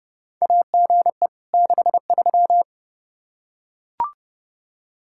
Morse code, heard as a keyed tone, about 30 wpm, 700 Hz. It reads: AGE 63
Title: K1USN SST QSO Practice 30wpm